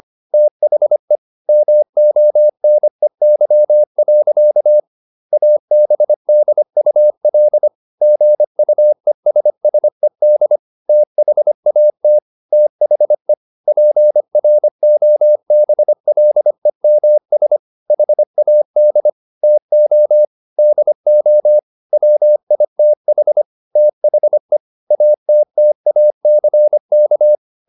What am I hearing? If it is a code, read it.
Morse code, heard as a keyed tone, about 25 wpm, 600 Hz. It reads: THE MONEY. ABDUL GUESSED THAT THE PROBLEMS HAD TO DO WITH THE ATTACK